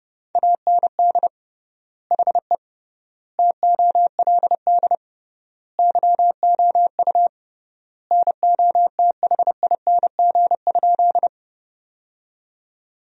Morse code, heard as a keyed tone, about 30 wpm, 700 Hz. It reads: AND HE TOLD YOU NOTHING?
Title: 3x Word-Spacing - Sentences from Top 500 Words - 30wpm